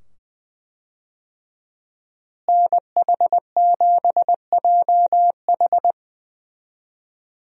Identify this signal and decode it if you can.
Morse code, heard as a keyed tone, about 20 wpm, 700 Hz. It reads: NH7JH
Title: USA Call Signs - 5 Characters - Repeated 2x - 20wpm